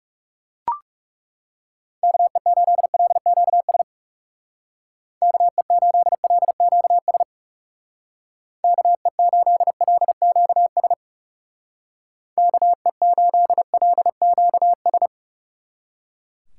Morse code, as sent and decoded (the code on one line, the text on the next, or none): -.- . ---.. .-.. --.- ... / -.- . ---.. .-.. --.- ... / -.- . ---.. .-.. --.- ... / -.- . ---.. .-.. --.- ...
KE8LQS KE8LQS KE8LQS KE8LQS